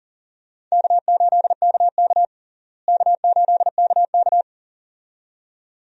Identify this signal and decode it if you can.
Morse code, heard as a keyed tone, about 37 wpm, 700 Hz. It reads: K8KK K8KK
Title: USA Call Signs - 4 Characters - Repeated 2x - 40wpm